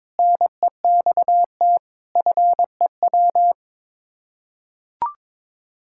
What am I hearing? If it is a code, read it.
Morse code, heard as a keyed tone, about 22 wpm, 700 Hz. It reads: NEXT FEW
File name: practice / 22wpm / Sets of 7 Words - 22wpm